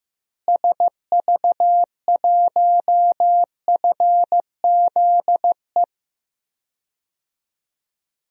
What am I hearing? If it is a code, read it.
Morse code, heard as a keyed tone, about 16 wpm, 700 Hz. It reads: SV1FZE
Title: DXCC-100 Call Signs - 6 Characters - Repeated 3x - 15wpm